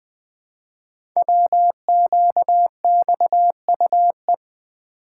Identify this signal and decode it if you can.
Morse code, heard as a keyed tone, about 20 wpm, 700 Hz. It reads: WQXUE
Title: Code Groups - 5 Letters - 20wpm